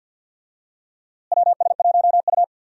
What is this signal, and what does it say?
Morse code, heard as a keyed tone, about 46 wpm, 700 Hz. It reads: WS1U
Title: USA Call Signs - 4 Characters - Repeated 2x - 50wpm